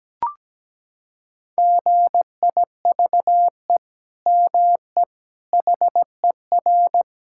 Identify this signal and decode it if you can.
Morse code, heard as a keyed tone, about 17 wpm, 700 Hz. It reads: GIVE ME HER
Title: Sentences from Top 1000 Words - Review Part 6 - 17wpm